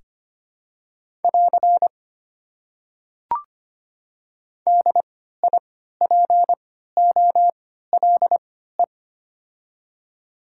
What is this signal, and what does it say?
Morse code, heard as a keyed tone, about 25 wpm, spaced as for 14 wpm, 690 Hz. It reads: <AR> DIPOLE
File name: Top 90 QSO Elements - 3x Character Spacing - 25wpm